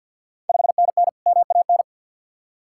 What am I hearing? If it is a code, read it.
Morse code, heard as a keyed tone, about 52 wpm, 700 Hz. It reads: 5NN MAN
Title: Ontario QSO Party Exchange 50wpm